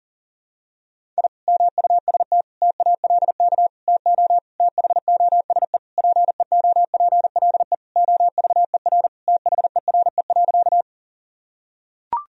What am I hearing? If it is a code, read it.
Morse code, heard as a keyed tone, about 40 wpm, 700 Hz. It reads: I MUST TALK TO THOSE PEOPLE OVER THERE.
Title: Sentences from Top 1000 Words - Review Part 1 - 40wpm